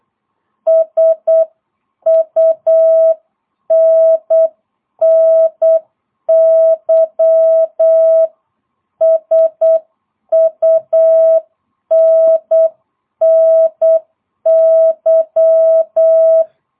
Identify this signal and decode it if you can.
Morse code, heard as a keyed tone, about 8 wpm, 650 Hz. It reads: SUNNY SUNNY
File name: jh8cbh / 2022cw